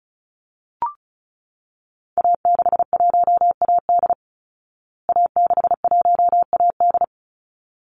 Morse code, heard as a keyed tone, about 34 wpm, 700 Hz. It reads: A61AD A61AD
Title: Call Signs Repeated 3x - 35wpm